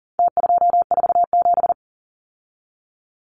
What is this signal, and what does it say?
Morse code, heard as a keyed tone, about 40 wpm, 700 Hz. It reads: T247
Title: Numbers - Contest Style - 4-Digit 40wpm